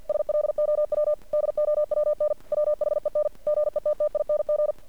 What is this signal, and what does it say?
Morse code, heard as a keyed tone, about 49 wpm, 600 Hz. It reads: SLOW DOWN WHEN GETTING
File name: archived / CRSnet